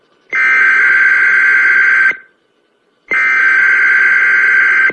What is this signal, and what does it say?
Emergency Alert System (SAME) tones (header bursts), heard as harsh buzzing bursts. It is heard from 0.3 to 4.4 s.